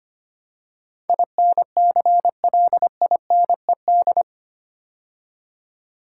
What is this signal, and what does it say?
Morse code, heard as a keyed tone, about 25 wpm, 700 Hz. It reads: INCLINED
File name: 8 Letter Words - First 3 Letters Given - 25wpm